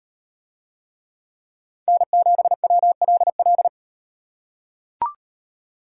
Morse code, heard as a keyed tone, about 38 wpm, 700 Hz. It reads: N7WLL